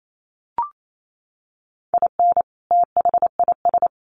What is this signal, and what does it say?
Morse code, heard as a keyed tone, about 28 wpm, 700 Hz. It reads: IN THIS
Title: Sets of 3 Words 28wpm